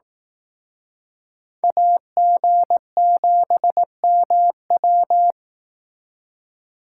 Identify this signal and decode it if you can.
Morse code, heard as a keyed tone, about 18 wpm, 700 Hz. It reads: AG7MW
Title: Speed Racing - USA 2x2 Call Signs 15wpm